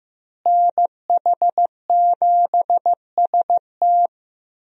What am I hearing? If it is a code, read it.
Morse code, heard as a keyed tone, about 15 wpm, 700 Hz. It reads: NH7ST
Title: USA Call Signs - 5 Characters - Repeated 3x - 15wpm